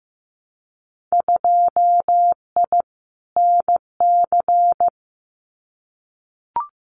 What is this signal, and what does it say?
Morse code, heard as a keyed tone, about 15 wpm, 700 Hz. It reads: 2I NC
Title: Winter Field Day Exchanges 15wpm